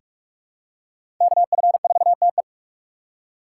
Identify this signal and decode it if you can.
Morse code, heard as a keyed tone, about 42 wpm, 700 Hz. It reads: KF4TE
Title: USA Call Signs - 3x Character Spacing - 45wpm